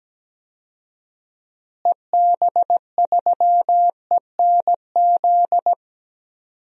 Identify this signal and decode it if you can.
Morse code, heard as a keyed tone, about 17 wpm, 700 Hz. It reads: EB3ENZ
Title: DXCC-100 Call Signs - 6 Characters - Repeated 2x - 17wpm